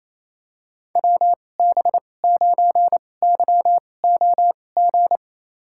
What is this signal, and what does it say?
Morse code, heard as a keyed tone, about 28 wpm, spaced as for 20 wpm, 700 Hz. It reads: WB9YOG
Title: USA Call Signs - 6 Characters - 2x Character Spacing - 28wpm